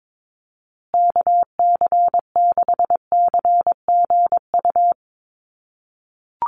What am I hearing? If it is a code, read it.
Morse code, heard as a keyed tone, about 22 wpm, 700 Hz. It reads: KC6CGU
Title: USA Call Signs Repeated 2x 22wpm